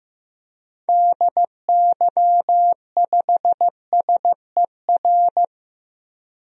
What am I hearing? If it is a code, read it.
Morse code, heard as a keyed tone, about 15 wpm, 690 Hz. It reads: DY5SER
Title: DXCC-100 Call Signs - 6 Characters - 15wpm